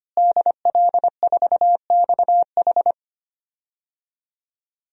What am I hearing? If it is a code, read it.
Morse code, heard as a keyed tone, about 25 wpm, 700 Hz. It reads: DL4XH
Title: DXCC-100 Call Signs - 5 Characters - Repeated 3x - 25wpm